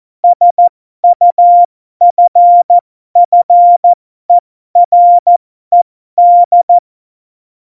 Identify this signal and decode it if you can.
Morse code, heard as a keyed tone, about 14 wpm, spaced as for 12 wpm, 700 Hz. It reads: SUFFERED
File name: Morsekurs / Witze